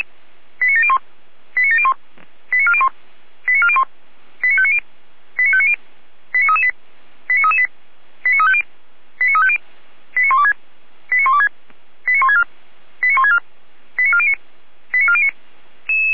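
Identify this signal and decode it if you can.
Five-tone selective call (ZVEI), heard as quick runs of five tones, tones of about 70 ms. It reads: 89701 89701 89401 89401 89590 89590 89338 89338 89350 89350 89126 89126 89164 89164 89409 89409
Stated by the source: ZVEI five-tone selective call received by SDR.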